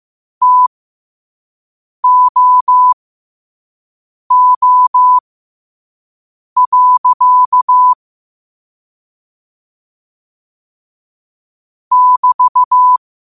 Morse code, heard as a keyed tone, about 15 wpm, spaced as for 5 wpm, 1000 Hz. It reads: TOO. <BT>
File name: qotdfiles / 5wpm